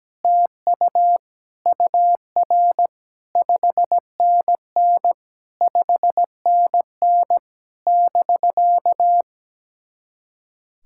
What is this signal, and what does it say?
Morse code, heard as a keyed tone, about 17 wpm, 700 Hz. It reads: TU UR 5NN 5NN <BK>
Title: SOTA QSO Practice 17wpm